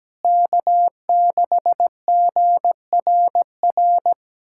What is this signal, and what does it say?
Morse code, heard as a keyed tone, about 17 wpm, 700 Hz. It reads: K6GRR
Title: USA Callsign DE Callsign - 5 Characters - 17wpm